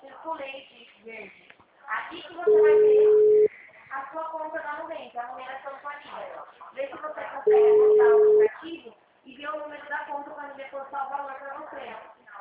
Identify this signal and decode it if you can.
Ringback tone (Europe, 425 Hz), with speech.